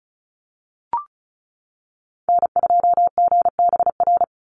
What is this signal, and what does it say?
Morse code, heard as a keyed tone, about 34 wpm, 700 Hz. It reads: N2GBR
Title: NA SOTA Chaser Callsigns 35wpm